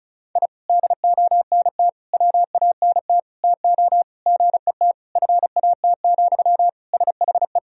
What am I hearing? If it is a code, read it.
Morse code, heard as a keyed tone, about 34 wpm, 700 Hz. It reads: I DONT WANT TO GET FAT, SHE
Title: Sentences from Top 1000 Words - Review Part 4 - 35wpm